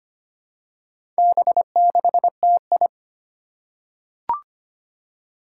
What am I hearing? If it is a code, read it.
Morse code, heard as a keyed tone, about 25 wpm, 700 Hz. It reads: B6TI
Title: Four Challenging Letter-Number Combinations 25wpm